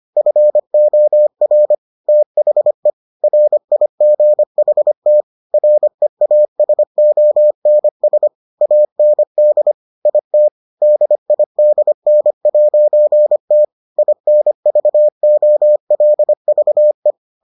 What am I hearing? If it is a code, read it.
Morse code, heard as a keyed tone, about 25 wpm, 600 Hz. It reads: FOR THE RIGHT REASONS AND IT DIDN'T INVOLVE